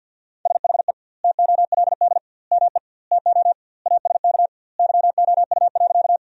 Morse code, heard as a keyed tone, about 49 wpm, 700 Hz. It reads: SHE TOLD ME TO ASK YOU.
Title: 2x Word-Spacing - Sentences from Top 400 Words - 50wpm